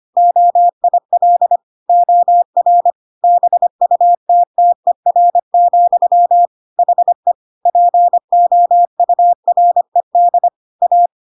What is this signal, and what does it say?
Morse code, heard as a keyed tone, about 25 wpm, 700 Hz. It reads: OIL OR BUTTER, HE POURED A
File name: Pinocchio / Pinocchio ch5 25 25 ik1hge